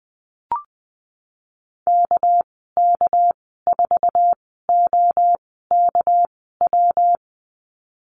Morse code, heard as a keyed tone, about 20 wpm, spaced as for 15 wpm, 700 Hz. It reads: KK4OKW
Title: 231 Lesson 37g - Letter x - Call Signs 20wpm